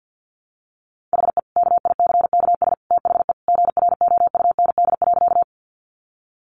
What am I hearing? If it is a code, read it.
Morse code, heard as a keyed tone, about 50 wpm, 700 Hz. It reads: HE KICKS THE GROUND.
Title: Additional Easy Phrases 50wpm